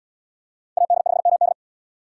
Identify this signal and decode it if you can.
Morse code, heard as a keyed tone, about 77 wpm, 690 Hz. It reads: AD5MD